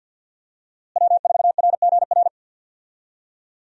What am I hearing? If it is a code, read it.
Morse code, heard as a keyed tone, about 52 wpm, 700 Hz. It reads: W4RGR
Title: USA Call Signs - 5 Characters - Repeated 3x - 50wpm